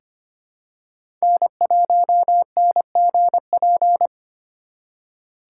Morse code, heard as a keyed tone, about 25 wpm, 700 Hz. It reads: N1NGP
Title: USA Call Signs - 5 Characters - 2x Character Spacing - 25wpm